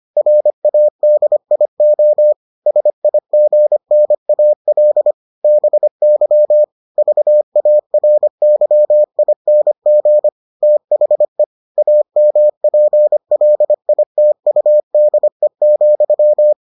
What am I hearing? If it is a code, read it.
Morse code, heard as a keyed tone, about 25 wpm, 600 Hz. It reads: RADIO SIGNAL BY VARYING THE AMPLITUDE,